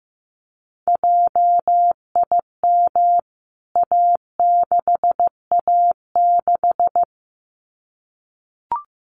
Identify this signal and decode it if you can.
Morse code, heard as a keyed tone, about 15 wpm, 700 Hz. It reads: JIM A6A6
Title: CWT Exchange 15wpm